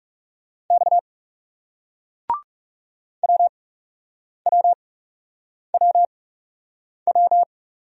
Morse code, heard as a keyed tone, about 40 wpm, spaced as for 8 wpm, 700 Hz. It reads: X WWWW